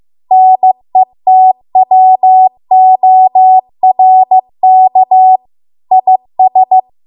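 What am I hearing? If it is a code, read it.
Morse code, heard as a keyed tone, about 15 wpm, 750 Hz. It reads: NETWORK IS
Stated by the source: Morse code practice at 15 wpm from the national amateur radio association's code archive.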